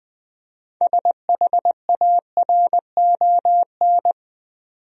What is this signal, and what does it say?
Morse code, heard as a keyed tone, about 20 wpm, 700 Hz. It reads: SHARON
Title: USA Common Names - 20wpm